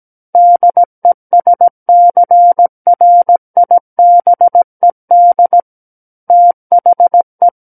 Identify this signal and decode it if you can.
Morse code, heard as a keyed tone, about 17 wpm, 700 Hz. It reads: DESCRIBED THE
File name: Book One / 08-Chapter 8 - 17 WPM